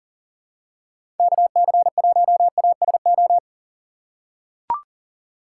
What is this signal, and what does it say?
Morse code, heard as a keyed tone, about 40 wpm, 700 Hz. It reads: KC1ASO
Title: USA Call Signs - 6 Characters - 3x Character Spacing - 40wpm